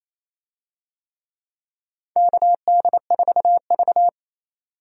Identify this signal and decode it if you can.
Morse code, heard as a keyed tone, about 27 wpm, 700 Hz. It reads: KD4V
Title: USA Call Signs - 4 Characters - 28wpm